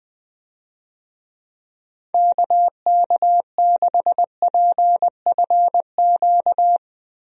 Morse code, heard as a keyed tone, about 20 wpm, 700 Hz. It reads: KK6PFQ
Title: USA Call Signs - 6 Characters - Repeated 3x - 20wpm